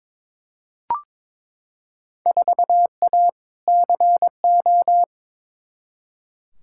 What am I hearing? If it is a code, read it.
Morse code, heard as a keyed tone, about 22 wpm, 700 Hz. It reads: 4A CO
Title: Field Day Exchanges - 22wpm